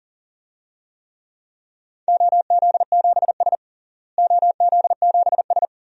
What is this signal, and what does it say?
Morse code, heard as a keyed tone, about 40 wpm, 700 Hz. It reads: OZ7S OZ7S
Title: DXCC-100 Call Signs - 4 Characters - Repeated 2x - 40wpm